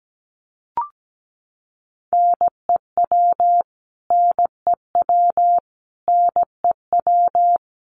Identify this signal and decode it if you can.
Morse code, heard as a keyed tone, about 17 wpm, 700 Hz. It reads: NEW NEW NEW
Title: Top 500 Words Repeated 17wpm